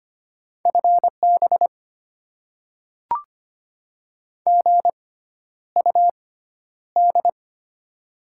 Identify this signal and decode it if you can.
Morse code, heard as a keyed tone, about 25 wpm, spaced as for 11 wpm, 700 Hz. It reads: FB GUD
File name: Top 20 QSO Elements - 6x Character Spacing - 25wpm